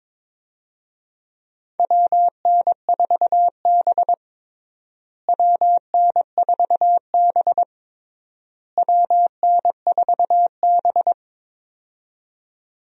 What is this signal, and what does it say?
Morse code, heard as a keyed tone, about 22 wpm, 700 Hz. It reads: WN4B WN4B WN4B
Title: USA Call Signs - 4 Characters - Repeated 3x - 22wpm